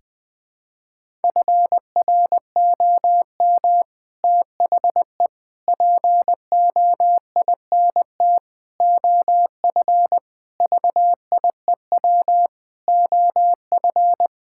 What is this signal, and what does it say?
Morse code, heard as a keyed tone, about 21 wpm, 700 Hz. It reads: FROM THE POINT OF VIEW OF
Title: Sentences with 6-Word Phrases - 20wpm